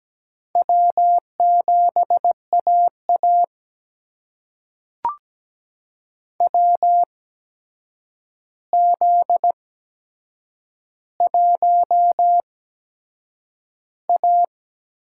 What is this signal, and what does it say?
Morse code, heard as a keyed tone, about 17 wpm, spaced as for 7 wpm, 700 Hz. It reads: W7AA WZ1A
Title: USA Call Signs - 4 Characters - 8x Character Spacing - 17wpm